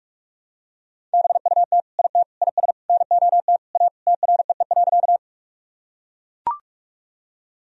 Morse code, heard as a keyed tone, about 41 wpm, 700 Hz. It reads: BUT IT IS NOT A TREE.
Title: Sentences from Top 1000 Words - Review Part 1 - 45wpm